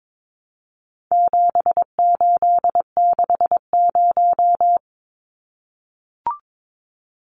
Morse code, heard as a keyed tone, about 22 wpm, 700 Hz. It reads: 7860
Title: Numbers - 4-Digit 22wpm